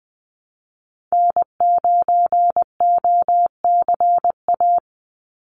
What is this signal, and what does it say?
Morse code, heard as a keyed tone, about 20 wpm, 700 Hz. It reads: N9OCA
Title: USA Call Signs 20wpm